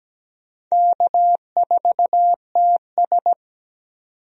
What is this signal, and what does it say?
Morse code, heard as a keyed tone, about 17 wpm, 700 Hz. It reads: K4TS
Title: USA Call Signs - 4 Characters - 17wpm